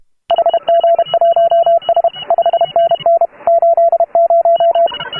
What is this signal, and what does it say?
Morse code, heard as a keyed tone, about 31 wpm, 690 Hz, with radio noise. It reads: HG1S 5NN 80
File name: AUDIO2018 / HA3FMR CW